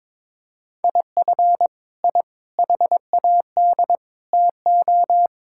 Morse code, heard as a keyed tone, about 22 wpm, 700 Hz. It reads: IF I HAD TO
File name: Sets of 9 Words - 22wpm